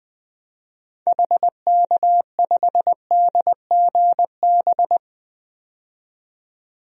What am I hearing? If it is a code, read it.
Morse code, heard as a keyed tone, about 20 wpm, 700 Hz. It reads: HK5DGB
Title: DXCC-100 Call Signs - 6 Characters - 20wpm